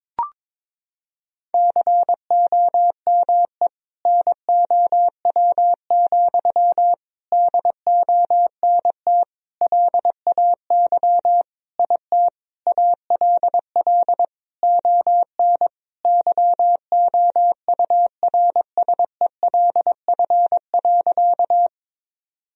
Morse code, heard as a keyed tone, about 22 wpm, 700 Hz. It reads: COME NOW, DONT LAY IT ALL ON YOURSELF.
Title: Sentences from Top 1000 Words - Review Part 2 - 22wpm